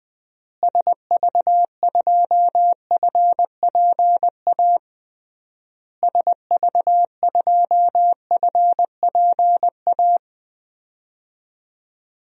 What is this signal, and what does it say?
Morse code, heard as a keyed tone, about 20 wpm, 700 Hz. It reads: SV2FPA SV2FPA
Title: DXCC-100 Call Signs - 6 Characters - Repeated 3x - 20wpm